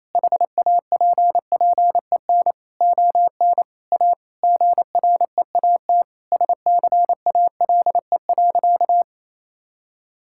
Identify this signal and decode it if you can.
Morse code, heard as a keyed tone, about 28 wpm, 700 Hz. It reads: HAPPEN ON A GREAT SCALE.
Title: Sentences from Top 1000 Words - Review Part 5 - 28wpm